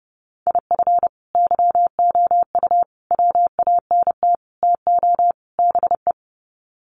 Morse code, heard as a keyed tone, about 29 wpm, 700 Hz. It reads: IF YOU WANT TO BE
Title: Sets of 5 Words Encore 30wpm